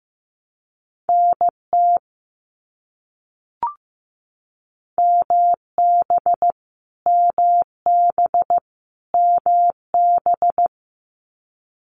Morse code, heard as a keyed tone, about 15 wpm, 700 Hz. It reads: NT MB MB MB
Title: Canadian Provinces and Territories - Repeated 3x - 15wpm